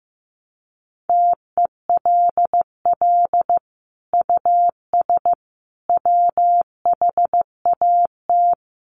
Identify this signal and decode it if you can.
Morse code, heard as a keyed tone, about 14 wpm, 700 Hz. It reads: TELL US WHAT
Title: Sentences from Top 1000 Words 15wpm